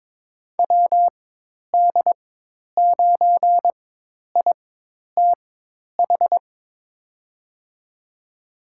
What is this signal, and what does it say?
Morse code, heard as a keyed tone, about 23 wpm, spaced as for 10 wpm, 700 Hz. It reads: WD9ITH
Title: USA Call Signs - 4x Character Spacing - 22wpm